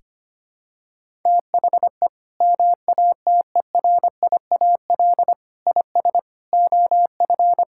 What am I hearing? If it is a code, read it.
Morse code, heard as a keyed tone, about 25 wpm, 700 Hz. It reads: THE MATERIAL IS OF
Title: Sentences from Top 1000 Words - Review Part 5 - 25wpm